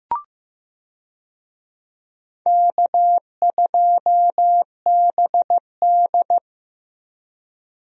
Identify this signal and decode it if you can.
Morse code, heard as a keyed tone, about 15 wpm, 700 Hz. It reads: K2BD
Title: USA Call Signs - 4 Characters - 15wpm